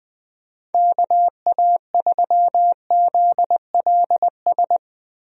Morse code, heard as a keyed tone, about 20 wpm, 700 Hz. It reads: KA3ZLS
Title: USA Call Signs - 6 Characters - Repeated 3x - 20wpm